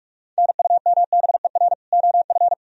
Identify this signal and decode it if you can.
Morse code, heard as a keyed tone, about 41 wpm, 700 Hz. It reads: NUMBER OF